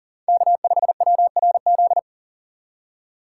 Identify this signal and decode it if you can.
Morse code, heard as a keyed tone, about 40 wpm, 700 Hz. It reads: K5WRZ
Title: USA Call Signs - 5 Characters - 3x Character Spacing - 40wpm